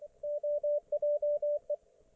Morse code, heard as a keyed tone, about 24 wpm, 590 Hz, with radio noise. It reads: OJE
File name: Archiv / CUC212 / ok1fhi cul212